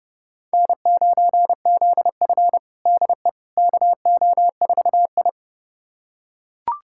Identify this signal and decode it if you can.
Morse code, heard as a keyed tone, about 30 wpm, 700 Hz. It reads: N9ZF DE KO4I
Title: USA Callsign DE Callsign - 4 Characters - 30wpm